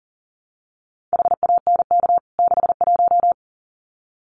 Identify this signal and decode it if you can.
Morse code, heard as a keyed tone, about 40 wpm, 700 Hz. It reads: HANK 61
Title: CWT Exchange 40wpm